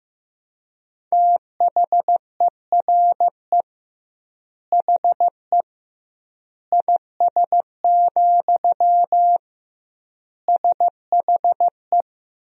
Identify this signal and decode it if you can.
Morse code, heard as a keyed tone, about 15 wpm, 700 Hz. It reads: THERE HE IS, SHE
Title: 2x Word-Spacing - Sentences from Top 100 Words - 15wpm